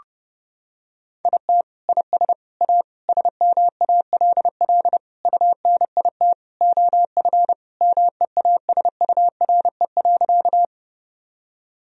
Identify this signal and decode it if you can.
Morse code, heard as a keyed tone, about 30 wpm, 700 Hz. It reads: IT IS A SMALL UNIT OF MEASURE.
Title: Sentences from Top 1000 Words - Review Part 2 - 30wpm